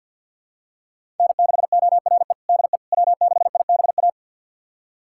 Morse code, heard as a keyed tone, about 51 wpm, 700 Hz. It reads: N6ORE DE W6IBA
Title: USA Callsign DE Callsign - 5 Characters - 50wpm